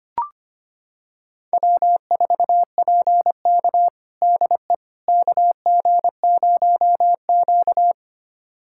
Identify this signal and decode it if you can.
Morse code, heard as a keyed tone, about 25 wpm, 700 Hz. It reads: W4PK DE KG0Q
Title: USA Callsign DE Callsign - 4 Characters - 25wpm